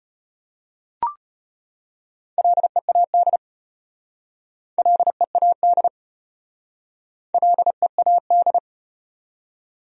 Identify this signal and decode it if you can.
Morse code, heard as a keyed tone, about 33 wpm, 700 Hz. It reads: LEAD LEAD LEAD